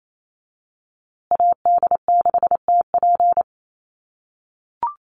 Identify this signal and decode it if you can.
Morse code, heard as a keyed tone, about 28 wpm, 700 Hz. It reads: AD6TP
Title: USA Call Signs Repeated 3x 28wpm